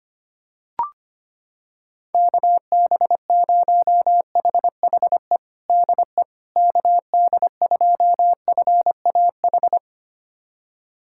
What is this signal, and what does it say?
Morse code, heard as a keyed tone, about 25 wpm, 700 Hz. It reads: KB0HHE DE KD2FAH
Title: USA Callsign DE Callsign 25wpm